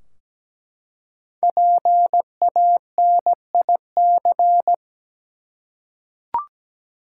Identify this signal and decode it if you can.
Morse code, heard as a keyed tone, about 17 wpm, 700 Hz. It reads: PANIC
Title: Five Letter Words - 17wpm